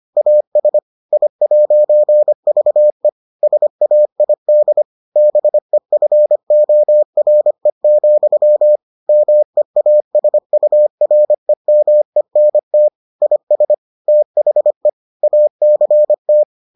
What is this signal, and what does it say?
Morse code, heard as a keyed tone, about 25 wpm, 600 Hz. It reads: AS I'VE SAID BEFORE, MEASUREMENT IS THE ACT